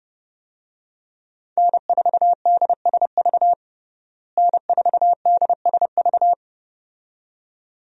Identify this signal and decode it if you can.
Morse code, heard as a keyed tone, about 30 wpm, 700 Hz. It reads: N4DSV N4DSV
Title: USA Call Signs - 5 Characters - Repeated 2x - 30wpm